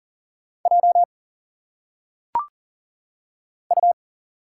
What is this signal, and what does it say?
Morse code, heard as a keyed tone, about 40 wpm, 700 Hz. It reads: J U
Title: Single Letters - Phonetic Pronunciation 40wpm